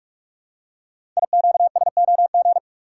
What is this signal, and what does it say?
Morse code, heard as a keyed tone, about 44 wpm, 700 Hz. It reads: IQSOG
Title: Code Groups - 5 Letters - 45wpm